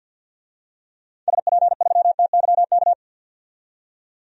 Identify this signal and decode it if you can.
Morse code, heard as a keyed tone, about 53 wpm, 700 Hz. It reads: SP3TYK
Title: DXCC-100 Call Signs - 6 Characters - 50wpm